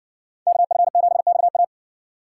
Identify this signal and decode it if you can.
Morse code, heard as a keyed tone, about 57 wpm, 700 Hz. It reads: KF7XA